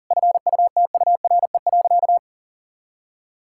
Morse code, heard as a keyed tone, about 40 wpm, 700 Hz. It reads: FUTURE.
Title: Sentences from Top 1000 Words - Review Part 6 - 40wpm